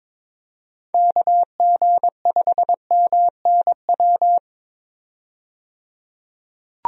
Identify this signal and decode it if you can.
Morse code, heard as a keyed tone, about 22 wpm, 700 Hz. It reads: KG5MNW
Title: USA Call Signs - 6 Characters - 22wpm